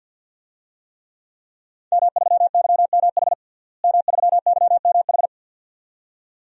Morse code, heard as a keyed tone, about 46 wpm, 700 Hz. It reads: M3YMH M3YMH
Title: DXCC-100 Call Signs - 5 Characters - Repeated 2x - 50wpm